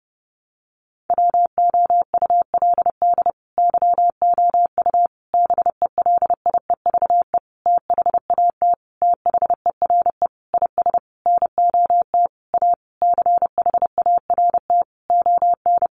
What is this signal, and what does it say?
Morse code, heard as a keyed tone, about 30 wpm, 700 Hz. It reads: WOULD YOU BELIEVE THAT THERE IS NOT A CHART ON
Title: Sentences from Top 1300 Words 30wpm